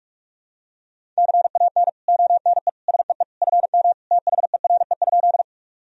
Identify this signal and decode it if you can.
Morse code, heard as a keyed tone, about 42 wpm, 700 Hz. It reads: CAN ONE SEE FM THERE?
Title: Sentences from Top 100 Words and Abbreviations 45wpm